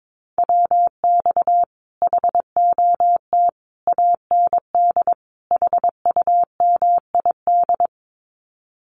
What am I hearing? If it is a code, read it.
Morse code, heard as a keyed tone, about 22 wpm, 700 Hz. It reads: WX HOT AND HUMID
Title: Common QSO Phrases Encore 22wpm